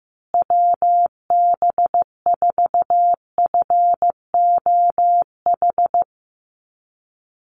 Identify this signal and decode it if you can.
Morse code, heard as a keyed tone, about 15 wpm, 700 Hz. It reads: WB4FOH
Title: USA Call Signs Repeated 3x 15wpm